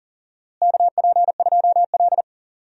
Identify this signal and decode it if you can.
Morse code, heard as a keyed tone, about 40 wpm, 700 Hz. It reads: KP2L
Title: DXCC-100 Call Signs - 4 Characters - Repeated 2x - 40wpm